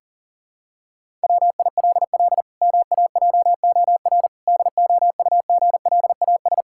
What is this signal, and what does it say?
Morse code, heard as a keyed tone, about 37 wpm, 700 Hz. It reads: WILL MAJOR DOUGLAS
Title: Pangram Challenge v2 - 40wpm